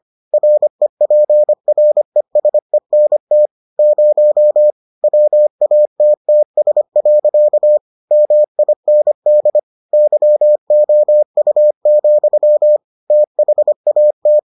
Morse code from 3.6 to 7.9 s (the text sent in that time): 0 WATTS.